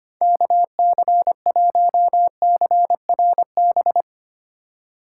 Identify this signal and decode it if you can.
Morse code, heard as a keyed tone, about 25 wpm, 700 Hz. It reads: KC1CRB
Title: USA Call Signs - 8x Character Spacing - 25wpm